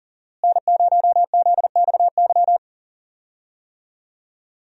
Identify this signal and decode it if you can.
Morse code, heard as a keyed tone, about 39 wpm, 700 Hz. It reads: N0ZXY
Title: USA Call Signs - 5 Characters - Repeated 3x - 40wpm